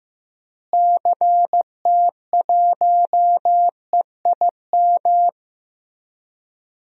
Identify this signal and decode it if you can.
Morse code, heard as a keyed tone, about 15 wpm, 700 Hz. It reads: CT1EIM
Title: DXCC-100 Call Signs - 6 Characters - Repeated 3x - 15wpm